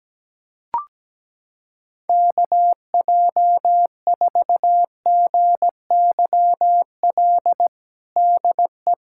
Morse code, heard as a keyed tone, about 17 wpm, 700 Hz. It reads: KJ4GYL DE
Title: USA Callsign DE Callsign - 6 Characters - 17wpm